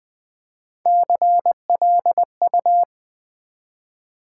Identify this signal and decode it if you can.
Morse code, heard as a keyed tone, about 20 wpm, 700 Hz. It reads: CLU
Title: Build A Word - 1 Letter At A Time - 4-Letter Words - 20wpm